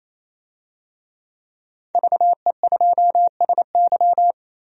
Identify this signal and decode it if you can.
Morse code, heard as a keyed tone, about 28 wpm, 700 Hz. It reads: VE2SY
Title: DXCC-100 Call Signs - 5 Characters - 28wpm